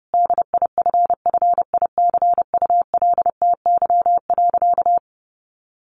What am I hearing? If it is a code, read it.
Morse code, heard as a keyed tone, about 30 wpm, 700 Hz. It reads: DIFFICULTY.
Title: Sentences from Top 1500 Words 30wpm